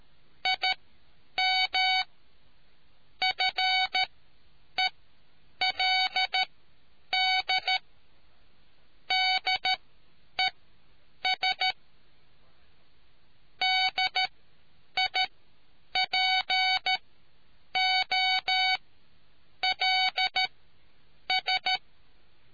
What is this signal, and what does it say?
Morse code, heard as a keyed tone, about 13 wpm, spaced as for 8 wpm, 750 Hz. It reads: IM FELD DES DIPOLS